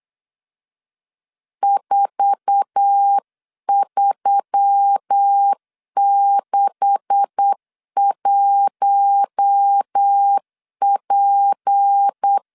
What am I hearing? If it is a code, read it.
Morse code, heard as a keyed tone, about 8 wpm, 800 Hz. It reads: 4361P
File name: CW / DET / 1 9 1500C S B 8wpm 5624